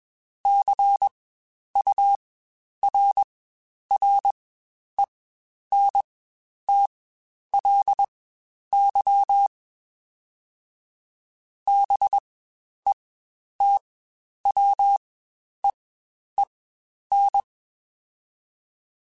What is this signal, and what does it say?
Morse code, heard as a keyed tone, about 21 wpm, spaced as for 10 wpm, 800 Hz. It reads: CURRENTLY BETWEEN
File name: Spaced Characters / Spaced Characters-mp3s / Weather5 22 sp @20